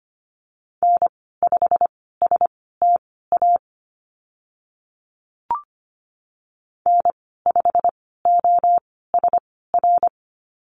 Morse code, heard as a keyed tone, about 25 wpm, spaced as for 16 wpm, 700 Hz. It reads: N5STA N5OSR
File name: Lesson 15g - Number 5 - Call Signs 25wpm